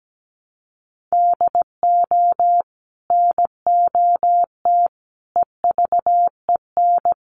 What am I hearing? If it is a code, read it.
Morse code, heard as a keyed tone, about 17 wpm, 700 Hz. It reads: DO NOT EVEN
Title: Sets of 4 Words 17wpm